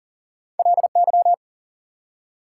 Morse code, heard as a keyed tone, about 40 wpm, 700 Hz. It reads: LY
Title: Prefixes and Suffixes 40wpm